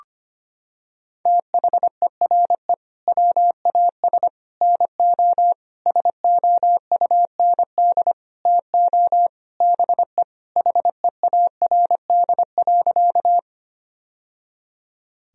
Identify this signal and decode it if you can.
Morse code, heard as a keyed tone, about 25 wpm, 700 Hz. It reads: THERE WAS NO SOUND TO BE HEARD.
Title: Sentences from Top 1000 Words - Review Part 1 - 25wpm